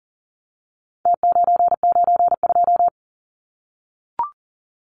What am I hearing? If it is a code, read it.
Morse code, heard as a keyed tone, about 40 wpm, 700 Hz. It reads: T992
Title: Numbers - Contest Style - 4-Digit 40wpm